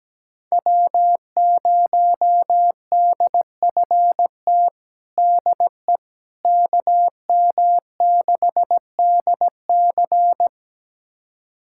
Morse code, heard as a keyed tone, about 17 wpm, 700 Hz. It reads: W0DFT DE KM6DC
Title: USA Callsign DE Callsign - 5 Characters - 17wpm